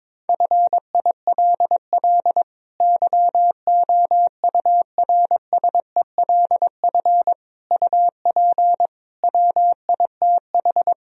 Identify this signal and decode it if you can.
Morse code, heard as a keyed tone, about 22 wpm, 700 Hz. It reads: FILL YOURSELF UP WITH
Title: Sentences from Top 1000 Words - Review Part 3 - 22wpm